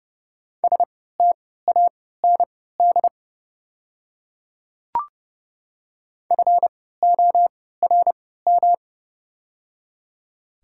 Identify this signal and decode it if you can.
Morse code, heard as a keyed tone, about 29 wpm, spaced as for 17 wpm, 700 Hz. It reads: STAND FORM
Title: 3x Character Spacing - Top 500 Words - 30wpm